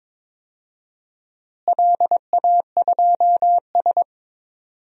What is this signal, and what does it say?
Morse code, heard as a keyed tone, about 22 wpm, 700 Hz. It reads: LA2S
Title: DXCC-100 Call Signs - 4 Characters - Repeated 2x - 22wpm